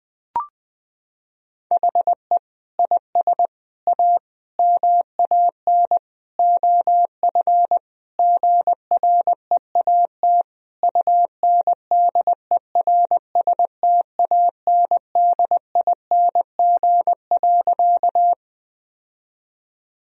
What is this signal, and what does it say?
Morse code, heard as a keyed tone, about 20 wpm, 700 Hz. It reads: HE IS A MAN OF GREAT UNDERSTANDING.
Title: Sentences from Top 1000 Words - Review Part 3 - 20wpm